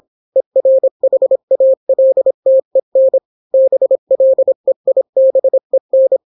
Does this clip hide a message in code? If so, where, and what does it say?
Yes: Morse code from 0.4 to 6.2 s, heard as a keyed tone, about 26 wpm, 530 Hz. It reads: ERHALTEN BLEIBEN